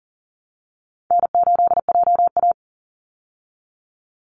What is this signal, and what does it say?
Morse code, heard as a keyed tone, about 40 wpm, 700 Hz. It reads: N8JA
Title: Call Signs Repeated 2x - 40wpm